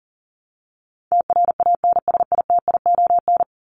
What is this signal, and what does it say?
Morse code, heard as a keyed tone, about 40 wpm, 700 Hz. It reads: TRANSITION
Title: Top 1751-2000 Words 40wpm